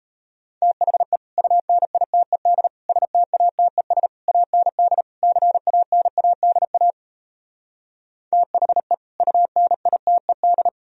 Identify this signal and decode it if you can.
Morse code, heard as a keyed tone, about 37 wpm, 700 Hz. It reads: THE UNITED STATES AND CANADA THE UNITED